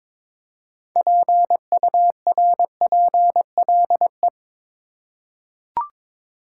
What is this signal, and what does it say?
Morse code, heard as a keyed tone, about 21 wpm, 700 Hz. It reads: PURPLE
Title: Six Letter Words - 22wpm